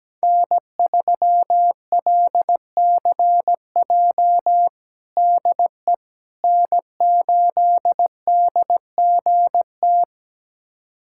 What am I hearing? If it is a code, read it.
Morse code, heard as a keyed tone, about 17 wpm, 700 Hz. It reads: N3LCJ DE N8DGT
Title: USA Callsign DE Callsign 17wpm